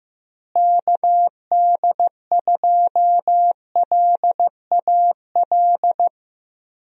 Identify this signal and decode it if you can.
Morse code, heard as a keyed tone, about 15 wpm, 700 Hz. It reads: KD2LAL
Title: USA Call Signs - 6 Characters - Repeated 2x - 15wpm